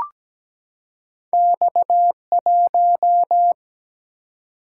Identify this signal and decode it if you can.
Morse code, heard as a keyed tone, about 17 wpm, 700 Hz. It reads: X1
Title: Single-Letter Single-Number 17wpm